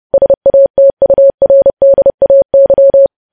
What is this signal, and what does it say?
Morse code, heard as a keyed tone, about 30 wpm, 550 Hz. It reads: SATURDAY
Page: news headlines in morse code at 30 wpm